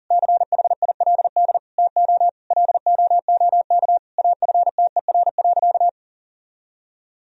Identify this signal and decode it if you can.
Morse code, heard as a keyed tone, about 37 wpm, 700 Hz. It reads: CHILD TO LOOK AFTER.
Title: Sentences from Top 1000 Words - Review Part 3 - 40wpm